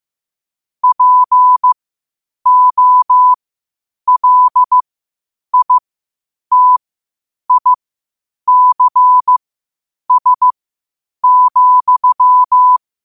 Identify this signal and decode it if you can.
Morse code, heard as a keyed tone, about 15 wpm, spaced as for 8 wpm, 1000 Hz. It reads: POLITICS,
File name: qotdfiles / 8wpm